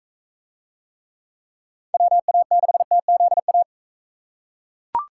Morse code, heard as a keyed tone, about 42 wpm, 700 Hz. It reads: WA6TZA